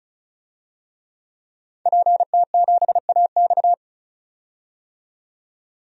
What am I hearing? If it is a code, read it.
Morse code, heard as a keyed tone, about 35 wpm, 700 Hz. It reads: PT7AX
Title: DXCC-100 Call Signs - 5 Characters - 35wpm